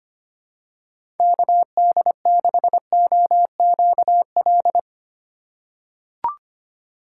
Morse code, heard as a keyed tone, about 25 wpm, 700 Hz. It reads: KD6OQL
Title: USA Call Signs - 6x Character Spacing - 25wpm